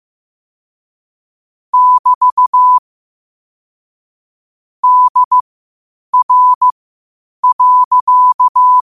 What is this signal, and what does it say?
Morse code, heard as a keyed tone, about 15 wpm, spaced as for 8 wpm, 1000 Hz. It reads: <BT> DR.
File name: qotdfiles / 8wpm